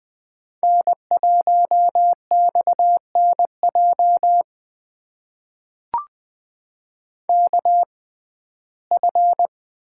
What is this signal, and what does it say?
Morse code, heard as a keyed tone, about 20 wpm, spaced as for 14 wpm, 700 Hz. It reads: N1XNJ KF
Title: USA Call Signs - 5 Characters - 6x Character Spacing - 20wpm